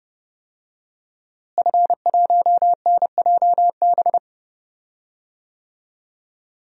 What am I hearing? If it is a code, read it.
Morse code, heard as a keyed tone, about 30 wpm, 700 Hz. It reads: F1NJB
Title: DXCC-100 Call Signs - 5 Characters - Repeated 2x - 30wpm